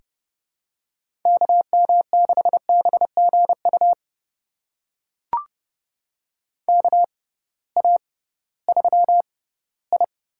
Morse code, heard as a keyed tone, about 32 wpm, spaced as for 18 wpm, 700 Hz. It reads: KM6BGU KA3I
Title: USA Call Signs - 6x Character Spacing - 30wpm